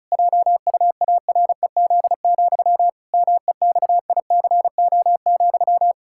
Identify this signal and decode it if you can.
Morse code, heard as a keyed tone, about 36 wpm, 700 Hz. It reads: JUAREZ, MEXICO,
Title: Pangram Challenge v2 - 35wpm